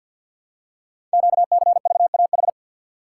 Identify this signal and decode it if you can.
Morse code, heard as a keyed tone, about 51 wpm, 700 Hz. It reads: QCVAH
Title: Code Groups - 5 Letters - 50wpm